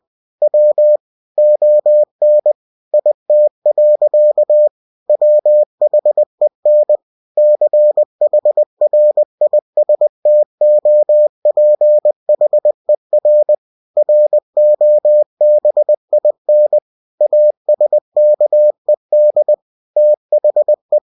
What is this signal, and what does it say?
Morse code, heard as a keyed tone, about 20 wpm, 600 Hz. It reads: W ON IT. WHEN CHRISTOPHER ROBIN ASKED THE
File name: cw-books / winnie-the-pooh / winnie-20wpm-page-0020